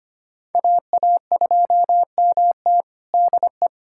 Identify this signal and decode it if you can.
Morse code, heard as a keyed tone, about 25 wpm, 700 Hz. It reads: AA2MT DE
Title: USA Callsign DE Callsign 25wpm